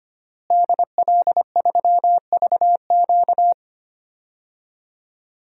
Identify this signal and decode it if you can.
Morse code, heard as a keyed tone, about 25 wpm, 700 Hz. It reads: DL3VQ
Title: DXCC-100 Call Signs - 5 Characters - Repeated 2x - 25wpm